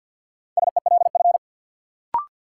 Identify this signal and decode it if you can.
Morse code, heard as a keyed tone, about 47 wpm, 700 Hz. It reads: SELF
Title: Sentences from Top 1400 Words - New Word Drill Alt - 50wpm